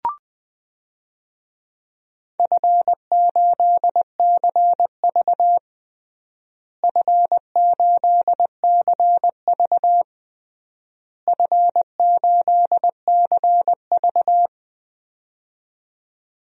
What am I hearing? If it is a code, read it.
Morse code, heard as a keyed tone, about 20 wpm, 700 Hz. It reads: F8CV F8CV F8CV
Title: DXCC-100 Call Signs - 4 Characters - Repeated 3x - 20wpm